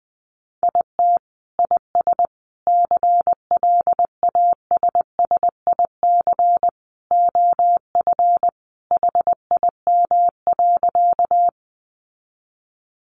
Morse code, heard as a keyed tone, about 19 wpm, 700 Hz. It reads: IT IS CLASSIC OF HIM.
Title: Sentences from Top 1000 Words 20wpm